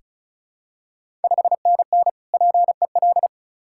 Morse code, heard as a keyed tone, about 36 wpm, 700 Hz. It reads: 5NN PEL
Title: Ontario QSO Party Exchange 35wpm